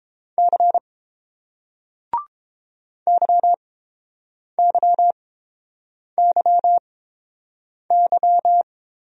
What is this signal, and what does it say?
Morse code, heard as a keyed tone, about 30 wpm, spaced as for 7 wpm, 700 Hz. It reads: C YYYY